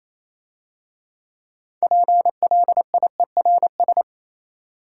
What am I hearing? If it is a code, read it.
Morse code, heard as a keyed tone, about 28 wpm, 690 Hz. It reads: PLIERS